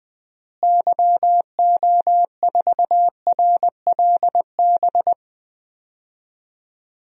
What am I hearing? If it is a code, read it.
Morse code, heard as a keyed tone, about 21 wpm, 700 Hz. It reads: YO4RLB
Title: DXCC-100 Call Signs - 6 Characters - Repeated 2x - 20wpm